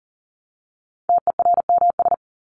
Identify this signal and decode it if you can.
Morse code, heard as a keyed tone, about 40 wpm, 700 Hz. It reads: TERMS
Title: ES:S Suffix - Top 100 Words with Base Words 40wpm